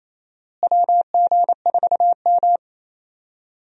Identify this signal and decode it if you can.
Morse code, heard as a keyed tone, about 27 wpm, 700 Hz. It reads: WG4M
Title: USA Call Signs - 4 Characters - Repeated 3x - 28wpm